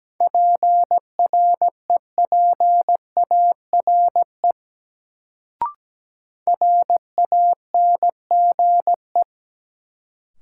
Morse code, heard as a keyed tone, about 17 wpm, 700 Hz. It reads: PREPARE RANGE
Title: Sentences from Top 900 Words - New Word Drill - 17wpm